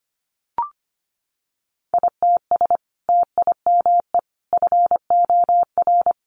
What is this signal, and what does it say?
Morse code, heard as a keyed tone, about 25 wpm, 700 Hz. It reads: ITS TIME FOR
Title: Sentences from Top 600 Words 25wpm